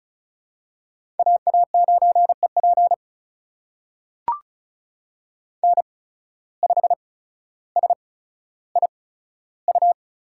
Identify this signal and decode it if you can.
Morse code, heard as a keyed tone, about 36 wpm, spaced as for 15 wpm, 700 Hz. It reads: AA9EP N5SIU
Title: USA Call Signs - 5 Characters - 8x Character Spacing - 35wpm